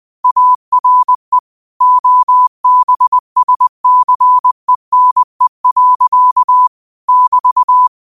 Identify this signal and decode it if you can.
Morse code, heard as a keyed tone, about 20 wpm, 1000 Hz. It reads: ARE OBSCENE. <BT>
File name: qotdfiles / 20wpm